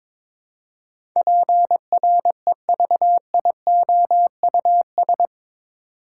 Morse code, heard as a keyed tone, about 22 wpm, 700 Hz. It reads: PREVIOUS
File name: practice / 22wpm / Eight Letter Words - 22wpm